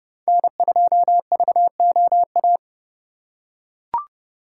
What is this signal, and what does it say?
Morse code, heard as a keyed tone, about 28 wpm, 700 Hz. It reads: N2VOA